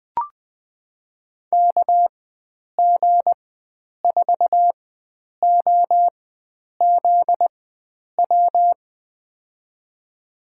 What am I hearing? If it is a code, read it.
Morse code, heard as a keyed tone, about 20 wpm, spaced as for 9 wpm, 700 Hz. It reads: KG4OZW
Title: USA Call Signs - 6 Characters - 4x Character Spacing - 20wpm